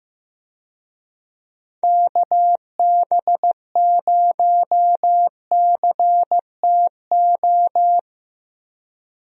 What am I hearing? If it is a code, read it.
Morse code, heard as a keyed tone, about 15 wpm, 700 Hz. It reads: KB0CTO
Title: USA Call Signs - 6 Characters - 15wpm